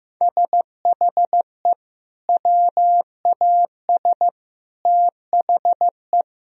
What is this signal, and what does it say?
Morse code, heard as a keyed tone, about 15 wpm, 700 Hz. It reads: SHE WAS THE
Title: Sentences from Top 1000 Words - Review Part 2 - 15wpm